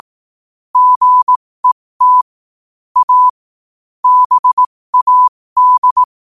Morse code, heard as a keyed tone, about 18 wpm, spaced as for 15 wpm, 1000 Hz. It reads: GET A BAD
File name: qotdfiles / 15wpm